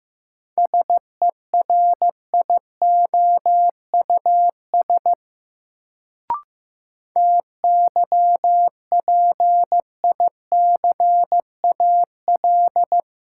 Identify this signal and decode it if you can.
Morse code, heard as a keyed tone, about 15 wpm, 700 Hz. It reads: SERIOUS TYPICAL
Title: Sentences from Top 1100 Words - New Word Drill - 15wpm